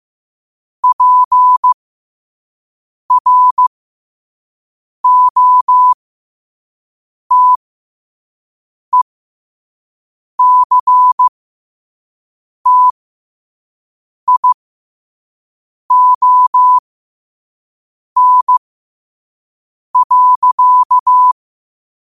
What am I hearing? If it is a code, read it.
Morse code, heard as a keyed tone, about 15 wpm, spaced as for 5 wpm, 1000 Hz. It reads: PROTECTION.
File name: qotdfiles / 5wpm